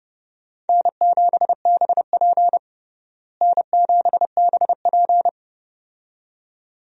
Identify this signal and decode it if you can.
Morse code, heard as a keyed tone, about 30 wpm, 700 Hz. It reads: N7BP N7BP
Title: USA Call Signs - 4 Characters - Repeated 2x - 30wpm